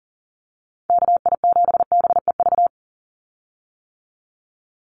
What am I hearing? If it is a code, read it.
Morse code, heard as a keyed tone, about 39 wpm, 700 Hz. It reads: KI7BEV
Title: USA Call Signs 40wpm